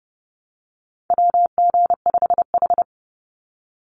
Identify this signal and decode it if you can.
Morse code, heard as a keyed tone, about 30 wpm, 700 Hz. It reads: WG5H
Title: Call Signs - 30wpm